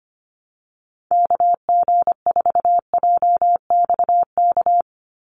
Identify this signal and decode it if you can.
Morse code, heard as a keyed tone, about 25 wpm, 700 Hz. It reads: KG4JXK
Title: USA Call Signs 25wpm